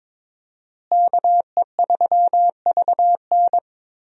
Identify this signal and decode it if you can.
Morse code, heard as a keyed tone, about 21 wpm, 700 Hz. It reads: KE3VN
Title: USA Call Signs - 5 Characters - Repeated 3x - 22wpm